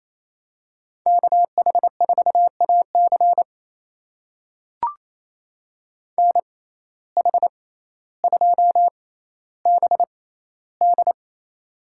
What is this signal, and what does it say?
Morse code, heard as a keyed tone, about 28 wpm, spaced as for 14 wpm, 700 Hz. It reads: KH4AC NH2BD
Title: USA Call Signs - 5 Characters - 6x Character Spacing - 28wpm